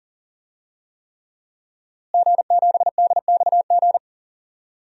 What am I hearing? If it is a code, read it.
Morse code, heard as a keyed tone, about 40 wpm, 700 Hz. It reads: G7DXG
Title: DXCC-100 Call Signs - 5 Characters - Repeated 3x - 40wpm